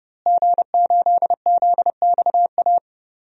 Morse code, heard as a keyed tone, about 31 wpm, 700 Hz. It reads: G8ZXA
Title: DXCC-100 Call Signs - 5 Characters - Repeated 3x - 30wpm